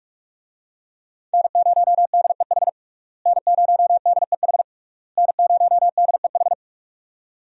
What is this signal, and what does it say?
Morse code, heard as a keyed tone, about 42 wpm, 700 Hz. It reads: N0DEH N0DEH N0DEH
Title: USA Call Signs - 5 Characters - Repeated 3x - 45wpm